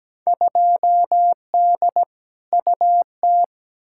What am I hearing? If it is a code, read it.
Morse code, heard as a keyed tone, about 17 wpm, 700 Hz. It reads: 2D UT
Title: Field Day Exchanges Encore - 17wpm